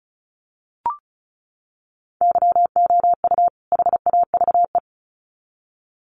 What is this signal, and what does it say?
Morse code, heard as a keyed tone, about 34 wpm, 700 Hz. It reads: YOU HAVE
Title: Sets of 2 Words 35wpm